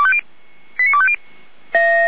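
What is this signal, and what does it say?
Five-tone selective call (ZVEI), heard as quick runs of five tones, tones of about 70 ms. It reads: 89250
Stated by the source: ZVEI five-tone call 89230 received on SDR, fire brigade Orlamünde (feuerwehr-orlamuende).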